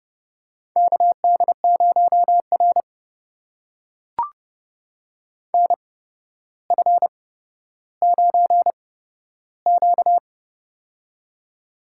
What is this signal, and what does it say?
Morse code, heard as a keyed tone, about 30 wpm, spaced as for 13 wpm, 700 Hz. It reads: KD0R NF9Q
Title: USA Call Signs - 4 Characters - 8x Character Spacing - 30wpm